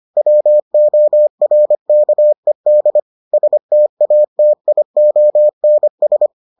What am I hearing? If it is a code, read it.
Morse code, heard as a keyed tone, about 25 wpm, 600 Hz. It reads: WORKED STATIONS